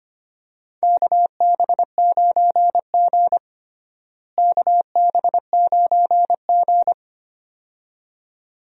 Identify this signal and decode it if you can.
Morse code, heard as a keyed tone, about 25 wpm, 700 Hz. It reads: KB9G KB9G
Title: USA Call Signs - 4 Characters - Repeated 2x - 25wpm